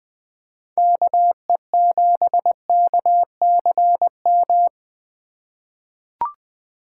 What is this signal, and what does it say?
Morse code, heard as a keyed tone, about 21 wpm, 700 Hz. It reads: KE7KCM
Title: USA Call Signs - 6 Characters - 2x Character Spacing - 20wpm